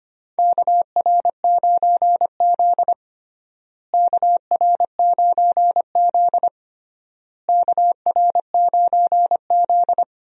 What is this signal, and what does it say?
Morse code, heard as a keyed tone, about 25 wpm, 700 Hz. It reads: KR9Z KR9Z KR9Z
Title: USA Call Signs - 4 Characters - Repeated 3x - 25wpm